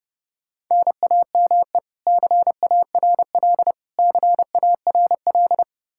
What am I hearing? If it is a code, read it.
Morse code, heard as a keyed tone, about 32 wpm, 700 Hz. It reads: NAME CARL CARL
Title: K1USN SST QSO Practice 30wpm